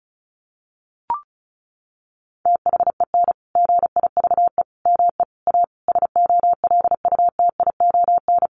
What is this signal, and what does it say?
Morse code, heard as a keyed tone, about 34 wpm, 700 Hz. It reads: THEN GIVE ME A SOLUTION
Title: Sentences from Top 1300 Words 35wpm